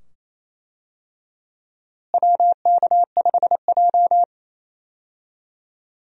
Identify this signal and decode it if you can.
Morse code, heard as a keyed tone, about 28 wpm, 700 Hz. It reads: WK5J
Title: USA Call Signs - 4 Characters - Repeated 2x - 28wpm